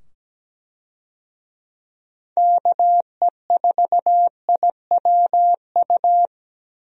Morse code, heard as a keyed tone, about 17 wpm, 700 Hz. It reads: KE4IWU
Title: USA Call Signs - 6 Characters - 17wpm